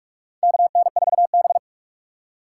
Morse code, heard as a keyed tone, about 44 wpm, 700 Hz. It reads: KN4B
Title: USA Call Signs - 4 Characters - 4x Character Spacing - 45wpm